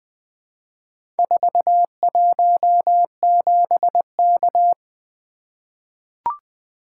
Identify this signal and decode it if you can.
Morse code, heard as a keyed tone, about 21 wpm, 700 Hz. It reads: 417K
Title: Four Letter-Number Combinations - 20wpm